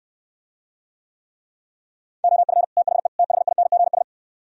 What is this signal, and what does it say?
Morse code, heard as a keyed tone, about 68 wpm, 700 Hz. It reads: OF THE THINGS